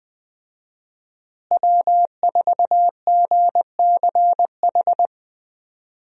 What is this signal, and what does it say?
Morse code, heard as a keyed tone, about 20 wpm, 700 Hz. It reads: W4GCH
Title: USA Call Signs - 5 Characters - Repeated 3x - 20wpm